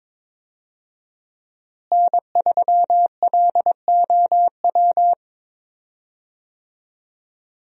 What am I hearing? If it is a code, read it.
Morse code, heard as a keyed tone, about 23 wpm, 700 Hz. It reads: N3LOW
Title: USA Call Signs - 5 Characters - Repeated 3x - 22wpm